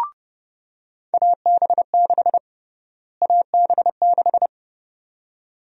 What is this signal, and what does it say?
Morse code, heard as a keyed tone, about 30 wpm, 700 Hz. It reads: AB6 AB6
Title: USA Call Sign Prefixes - Repeated 2x - 30wpm